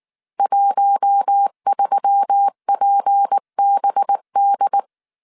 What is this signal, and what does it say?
Morse code, heard as a keyed tone, about 19 wpm, 800 Hz. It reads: 13PBD
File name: CW / DET / 1 6 1500C S C 18wpm 2356